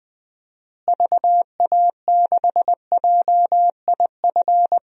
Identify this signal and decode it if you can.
Morse code, heard as a keyed tone, about 20 wpm, 700 Hz. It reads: VA6JIF
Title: DXCC-100 Call Signs - 6 Characters - Repeated 3x - 20wpm